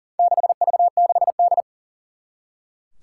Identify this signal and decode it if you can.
Morse code, heard as a keyed tone, about 40 wpm, 700 Hz. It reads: 6V6D
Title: Four Challenging Letter-Number Combinations 40wpm